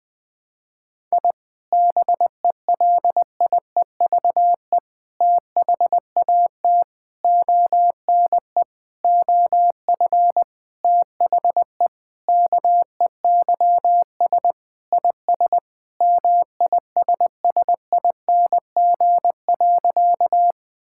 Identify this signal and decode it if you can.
Morse code, heard as a keyed tone, about 21 wpm, 700 Hz. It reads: I BELIEVE THAT ONE OF THE KEYS IS MISSING.
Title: Sentences with 4-Word Phrases - 20wpm